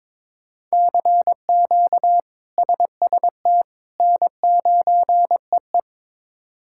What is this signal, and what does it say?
Morse code, heard as a keyed tone, about 22 wpm, 700 Hz. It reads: CQ SST N9EE
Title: K1USN SST QSO Practice - 22wpm